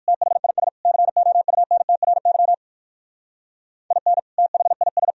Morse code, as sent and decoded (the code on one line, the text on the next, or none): - .... .. ... / -.-. --- ..- -. - .-. -.-- / .. -. / - .... .. ...
THIS COUNTRY IN THIS